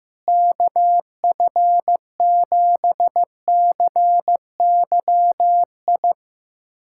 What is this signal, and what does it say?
Morse code, heard as a keyed tone, about 15 wpm, 700 Hz. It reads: KF7CYI
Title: USA Callsign DE Callsign - 6 Characters - 15wpm